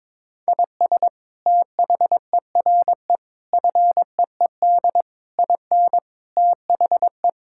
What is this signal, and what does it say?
Morse code, heard as a keyed tone, about 21 wpm, 700 Hz. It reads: IS THERE FEED IN THE
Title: Sentences from Top 1600 Words 22wpm